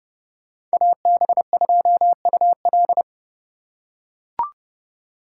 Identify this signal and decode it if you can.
Morse code, heard as a keyed tone, about 29 wpm, 700 Hz. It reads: AB2UL